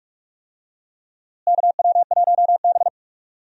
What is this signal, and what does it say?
Morse code, heard as a keyed tone, about 46 wpm, 700 Hz. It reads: KW1B
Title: USA Call Signs - 4 Characters - Repeated 3x - 45wpm